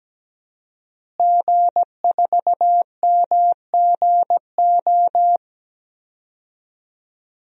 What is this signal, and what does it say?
Morse code, heard as a keyed tone, about 17 wpm, 700 Hz. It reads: G4MGO
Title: DXCC-100 Call Signs - 5 Characters - Repeated 2x - 17wpm